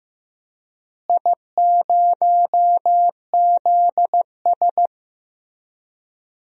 Morse code, heard as a keyed tone, about 15 wpm, 700 Hz. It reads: I0ZS
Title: DXCC-100 Call Signs - 4 Characters - Repeated 3x - 15wpm